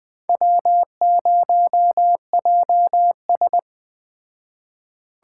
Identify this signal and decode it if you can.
Morse code, heard as a keyed tone, about 21 wpm, 700 Hz. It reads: W0JS